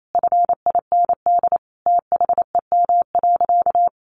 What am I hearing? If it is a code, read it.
Morse code, heard as a keyed tone, about 28 wpm, 700 Hz. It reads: FIND THEM.
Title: Sentences from Top 1500 Words 28wpm